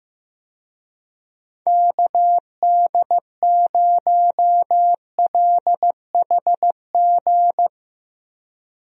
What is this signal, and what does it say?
Morse code, heard as a keyed tone, about 15 wpm, 700 Hz. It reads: KD0LHG
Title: USA Call Signs - 6 Characters - Repeated 2x - 15wpm